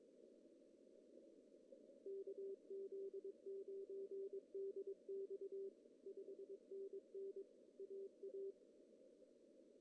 Morse code, heard as a keyed tone, about 22 wpm, 410 Hz. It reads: KZ9DX 5NN AA
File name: CW / cqww-cw-k5tr